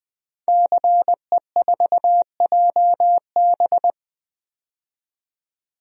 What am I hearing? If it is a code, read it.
Morse code, heard as a keyed tone, about 21 wpm, 700 Hz. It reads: CE4JB
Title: DXCC-100 Call Signs - 5 Characters - 20wpm